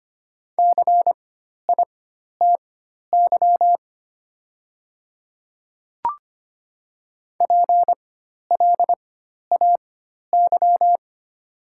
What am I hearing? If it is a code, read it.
Morse code, heard as a keyed tone, about 25 wpm, spaced as for 12 wpm, 700 Hz. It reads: CITY PLAY
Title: 4x Character Spacing - Top 300 Words - 25wpm